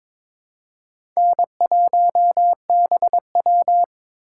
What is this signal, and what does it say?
Morse code, heard as a keyed tone, about 23 wpm, 700 Hz. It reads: N1BW
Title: USA Call Signs - 4 Characters - 2x Character Spacing - 22wpm